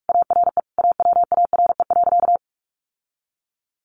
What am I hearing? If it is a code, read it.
Morse code, heard as a keyed tone, about 44 wpm, 700 Hz. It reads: ARE AWARE.
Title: Sentences from Top 1300 Words 45wpm